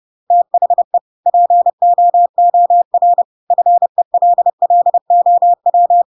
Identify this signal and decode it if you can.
Morse code, heard as a keyed tone, about 30 wpm, 700 Hz. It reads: THE POOR FELLOW
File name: Pinocchio / Pinocchio ch1 30 30 ik1hge